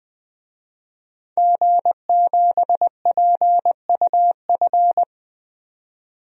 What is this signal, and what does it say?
Morse code, heard as a keyed tone, about 20 wpm, 700 Hz. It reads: G7PUF
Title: DXCC-100 Call Signs - 5 Characters - 20wpm